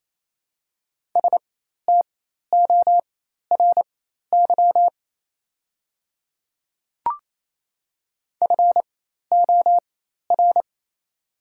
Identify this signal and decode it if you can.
Morse code, heard as a keyed tone, about 28 wpm, spaced as for 13 wpm, 690 Hz. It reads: STORY FOR
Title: 4x Character Spacing - Top 300 Words - 28wpm